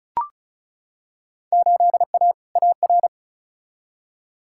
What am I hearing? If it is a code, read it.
Morse code, heard as a keyed tone, about 36 wpm, 700 Hz. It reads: 8A AR
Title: Field Day Exchanges Encore - 35wpm